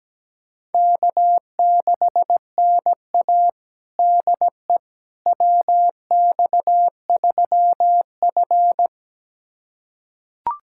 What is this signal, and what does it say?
Morse code, heard as a keyed tone, about 17 wpm, 700 Hz. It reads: K6NA DE WX3F
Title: USA Callsign DE Callsign - 4 Characters - 17wpm